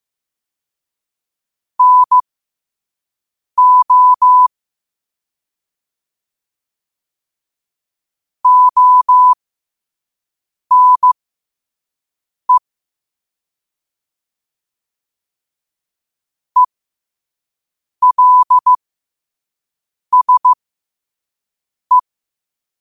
Morse code, heard as a keyed tone, about 15 wpm, spaced as for 5 wpm, 1000 Hz. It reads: NO ONE ELSE
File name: qotdfiles / 5wpm